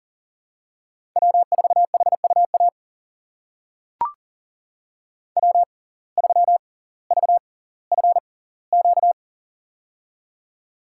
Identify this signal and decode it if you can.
Morse code, heard as a keyed tone, about 40 wpm, spaced as for 20 wpm, 700 Hz. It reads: W4HUA W3VFQ
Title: USA Call Signs - 6x Character Spacing - 40wpm